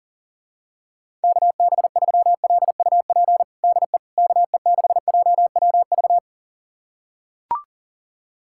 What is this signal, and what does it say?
Morse code, heard as a keyed tone, about 40 wpm, 700 Hz. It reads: KB3LUP DE KE6JWV
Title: USA Callsign DE Callsign 40wpm